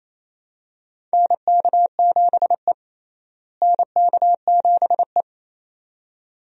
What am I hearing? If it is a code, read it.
Morse code, heard as a keyed tone, about 28 wpm, 700 Hz. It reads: NK7E NK7E
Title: USA Call Signs - 4 Characters - Repeated 2x - 28wpm